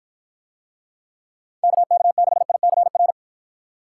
Morse code, heard as a keyed tone, about 51 wpm, 700 Hz. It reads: KK6ICR